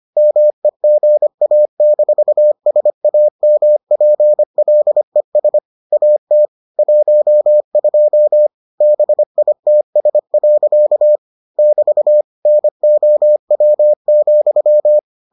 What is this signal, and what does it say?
Morse code, heard as a keyed tone, about 25 wpm, 600 Hz. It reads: MEGA-SAMPLES AT 12 BITS. <BT> NOW,